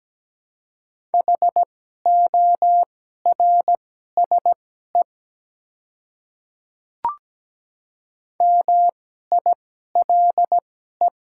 Morse code, heard as a keyed tone, about 17 wpm, spaced as for 12 wpm, 700 Hz. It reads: HORSE MILE
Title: 2x Character Spacing - Top 300 Words - 17wpm